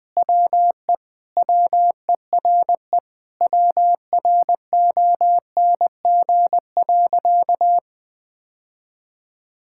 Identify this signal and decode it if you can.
Morse code, heard as a keyed tone, about 20 wpm, 700 Hz. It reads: WE WERE WRONG.
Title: Sentences with 5-Word Phrases - Spoken Beginning - 20wpm